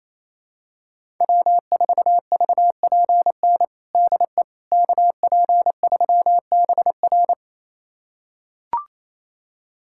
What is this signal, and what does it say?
Morse code, heard as a keyed tone, about 27 wpm, 700 Hz. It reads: W4VPN DE KP3BR
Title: USA Callsign DE Callsign - 5 Characters - 28wpm